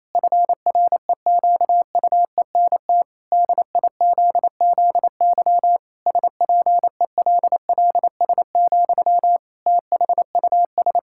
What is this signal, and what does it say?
Morse code, heard as a keyed tone, about 28 wpm, 700 Hz. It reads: FREQUENT DIZZY SPELLS, THUS
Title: Pangram Challenge v2 - 28wpm